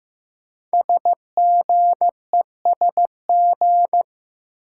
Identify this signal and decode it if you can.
Morse code, heard as a keyed tone, about 15 wpm, 700 Hz. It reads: SGESG
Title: Code Groups - 5 Letters - 15wpm